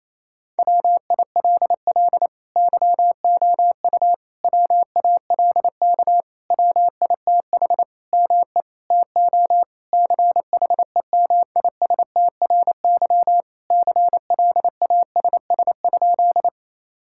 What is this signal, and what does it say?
Morse code, heard as a keyed tone, about 28 wpm, 700 Hz. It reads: WILL YOU WALK WITH ME TO CHEMISTRY CLASS?
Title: Sentences from Top 1000 Words - Review Part 6 - 28wpm